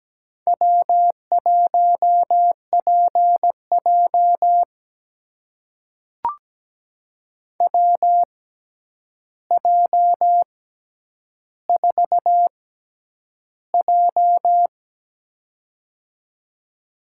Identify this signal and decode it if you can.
Morse code, heard as a keyed tone, about 17 wpm, spaced as for 9 wpm, 700 Hz. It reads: W1PJ WJ4J
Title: USA Call Signs - 4 Characters - 6x Character Spacing - 17wpm